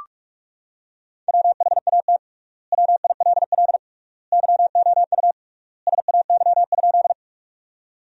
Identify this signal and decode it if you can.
Morse code, heard as a keyed tone, about 42 wpm, 700 Hz. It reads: WHAT WILL YOU SAY?
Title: 3x Word-Spacing - Sentences from Top 200 Words - 45wpm